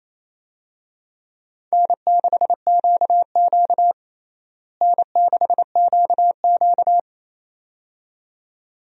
Morse code, heard as a keyed tone, about 28 wpm, 700 Hz. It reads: N6QQ N6QQ
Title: USA Call Signs - 4 Characters - Repeated 2x - 28wpm